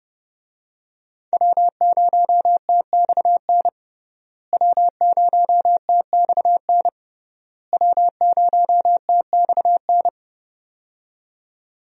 Morse code, heard as a keyed tone, about 32 wpm, 700 Hz. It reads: W0TXN W0TXN W0TXN
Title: USA Call Signs - 5 Characters - Repeated 3x - 30wpm